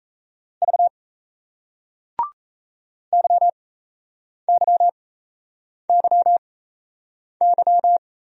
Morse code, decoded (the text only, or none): V YYYY